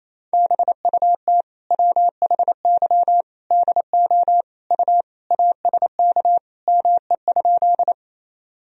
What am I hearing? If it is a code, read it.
Morse code, heard as a keyed tone, about 28 wpm, 700 Hz. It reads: BUT WHY DO U ASK ME?
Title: Sentences from Top 200 Words and Abbreviations 28wpm